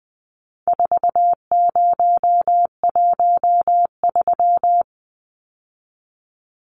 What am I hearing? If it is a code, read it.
Morse code, heard as a keyed tone, about 20 wpm, 700 Hz. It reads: 4013
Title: Numbers - 4-Digit 20wpm